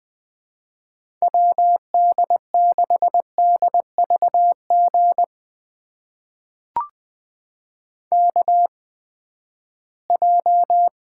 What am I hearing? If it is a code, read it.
Morse code, heard as a keyed tone, about 20 wpm, spaced as for 14 wpm, 700 Hz. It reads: WD6DVG KJ
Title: USA Call Signs - 6 Characters - 8x Character Spacing - 20wpm